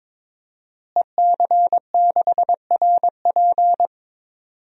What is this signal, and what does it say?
Morse code, heard as a keyed tone, about 21 wpm, 700 Hz. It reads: EC6RP
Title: DXCC-100 Call Signs - 5 Characters - Repeated 3x - 22wpm